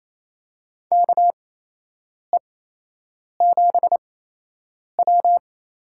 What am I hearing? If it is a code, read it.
Morse code, heard as a keyed tone, about 28 wpm, spaced as for 8 wpm, 700 Hz. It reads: KE7W
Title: USA Call Signs - 8x Character Spacing - 28wpm